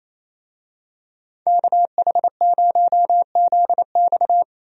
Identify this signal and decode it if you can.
Morse code, heard as a keyed tone, about 28 wpm, 700 Hz. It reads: KH0ZX
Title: USA Call Signs - 5 Characters - Repeated 3x - 28wpm